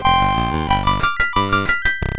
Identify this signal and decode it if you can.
There is no code.